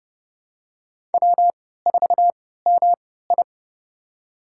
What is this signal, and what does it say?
Morse code, heard as a keyed tone, about 32 wpm, spaced as for 17 wpm, 700 Hz. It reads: W4MI
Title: USA Call Signs - 3x Character Spacing - 30wpm